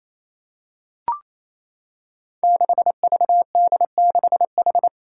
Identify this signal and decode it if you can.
Morse code, heard as a keyed tone, about 29 wpm, 700 Hz. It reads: 6VD6H
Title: Five Challenging Letter-Number Combinations 28wpm